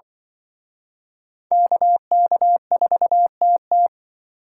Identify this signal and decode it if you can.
Morse code, heard as a keyed tone, about 24 wpm, 700 Hz. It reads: KK4TT
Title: Speed Racing - USA 2x2 Call Signs 20wpm